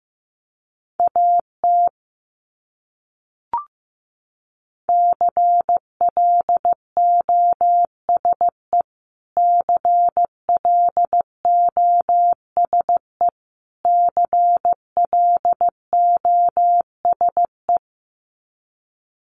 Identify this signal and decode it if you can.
Morse code, heard as a keyed tone, about 15 wpm, 700 Hz. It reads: AT CLOSE CLOSE CLOSE
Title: Top 500 Words Repeated 15wpm